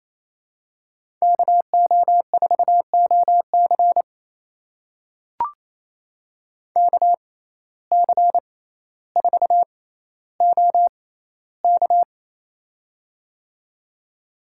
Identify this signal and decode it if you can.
Morse code, heard as a keyed tone, about 28 wpm, spaced as for 14 wpm, 700 Hz. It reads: KO4OC KC4OK
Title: USA Call Signs - 5 Characters - 6x Character Spacing - 28wpm